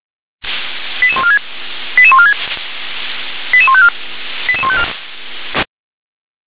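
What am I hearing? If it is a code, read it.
Five-tone selective call (ZVEI), heard as quick runs of five tones, tones of about 70 ms. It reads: 88146 88146 88154 88154
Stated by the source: ZVEI five-tone selective call received on an SDR.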